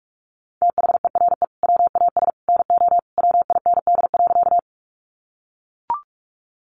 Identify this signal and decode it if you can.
Morse code, heard as a keyed tone, about 44 wpm, 700 Hz. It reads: THERE WAS NO WIND.
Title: Sentences from Top 400 Words 45wpm